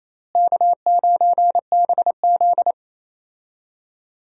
Morse code, heard as a keyed tone, about 28 wpm, 700 Hz. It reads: K9BZ
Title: USA Call Signs - 4 Characters - Repeated 3x - 28wpm